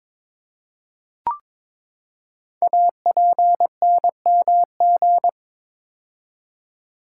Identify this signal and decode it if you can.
Morse code, heard as a keyed tone, about 22 wpm, 700 Hz. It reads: APNMG
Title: Code Groups - 5 Letters - 22wpm